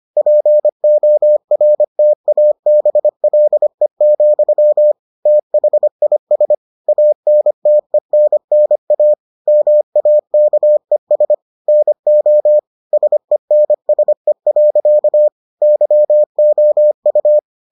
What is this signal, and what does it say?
Morse code, heard as a keyed tone, about 25 wpm, 600 Hz. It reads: PORTABLE, THIS ANTENNA MAKES NO SENSE. YOU